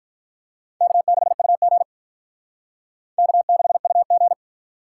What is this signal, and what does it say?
Morse code, heard as a keyed tone, about 44 wpm, 700 Hz. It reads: K6UG K6UG
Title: Speed Racing - USA Call Signs 35wpm